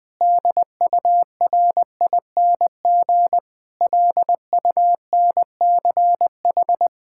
Morse code, heard as a keyed tone, about 20 wpm, 700 Hz. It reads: DURING LUNCH
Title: Difficult Sentences with Hints - 20wpm